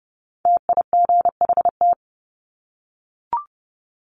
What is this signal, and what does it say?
Morse code, heard as a keyed tone, about 30 wpm, 700 Hz. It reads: TIGHT
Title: Top 1751-2000 Words 30wpm